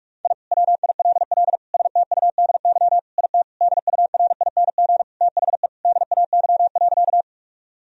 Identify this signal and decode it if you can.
Morse code, heard as a keyed tone, about 46 wpm, 700 Hz. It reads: I WILL STUDY IT DURING THE DAY.
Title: Sentences from Top 1000 Words - Review Part 2 - 45wpm